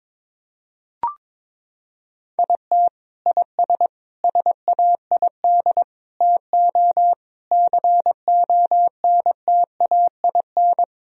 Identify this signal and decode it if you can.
Morse code, heard as a keyed tone, about 22 wpm, 700 Hz. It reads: IT IS SAID TO CONTAIN
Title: Sentences from Top 1000 Words - Review Part 2 - 22wpm